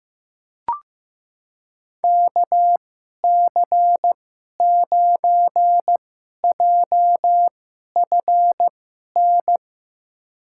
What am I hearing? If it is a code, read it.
Morse code, heard as a keyed tone, about 15 wpm, spaced as for 11 wpm, 700 Hz. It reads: KC9JFN
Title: USA Call Signs - 6 Characters - 2x Character Spacing - 15wpm